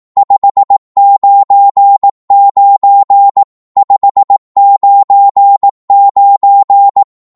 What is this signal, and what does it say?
Morse code, heard as a keyed tone, about 18 wpm, 800 Hz. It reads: 599 599
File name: Morse / qso-18wpm